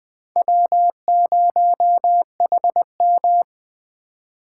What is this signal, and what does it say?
Morse code, heard as a keyed tone, about 20 wpm, 700 Hz. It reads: W0HM
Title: Speed Racing - USA 1x2 Call Signs 17wpm